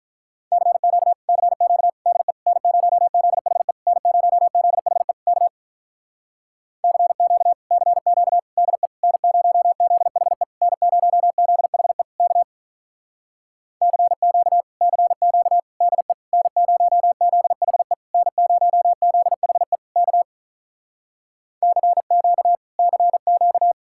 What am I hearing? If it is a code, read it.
Morse code, heard as a keyed tone, about 46 wpm, 700 Hz. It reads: CQ CQ DE N0ZHE N0ZHE K CQ CQ DE N0ZHE N0ZHE K CQ CQ DE N0ZHE N0ZHE K CQ CQ